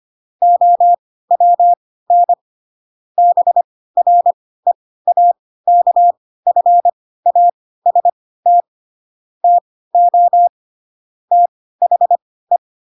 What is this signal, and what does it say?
Morse code, heard as a keyed tone, about 25 wpm, spaced as for 16 wpm, 700 Hz. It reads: OWN BREAKFAST TO THE